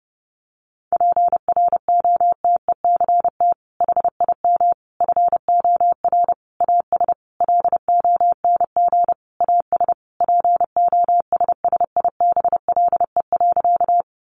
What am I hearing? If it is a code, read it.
Morse code, heard as a keyed tone, about 30 wpm, 700 Hz. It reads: PROTECT HIM FOR AS LONG AS POSSIBLE.
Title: Sentences from Top 900 Words 30wpm